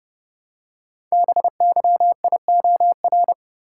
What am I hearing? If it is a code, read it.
Morse code, heard as a keyed tone, about 30 wpm, 700 Hz. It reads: BYIOR
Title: Code Groups - 5 Letters - 30wpm